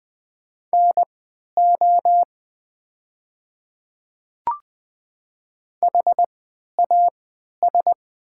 Morse code, heard as a keyed tone, about 20 wpm, spaced as for 11 wpm, 700 Hz. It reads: NO HAS
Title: 3x Character Spacing - Top 100 Words - 20wpm